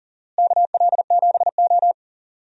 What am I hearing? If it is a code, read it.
Morse code, heard as a keyed tone, about 37 wpm, 700 Hz. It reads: KL7O
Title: DXCC-100 Call Signs - 4 Characters - Repeated 3x - 40wpm